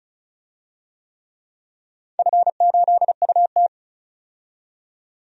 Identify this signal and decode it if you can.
Morse code, heard as a keyed tone, about 36 wpm, 700 Hz. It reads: F8UT
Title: DXCC-100 Call Signs - 4 Characters - 35wpm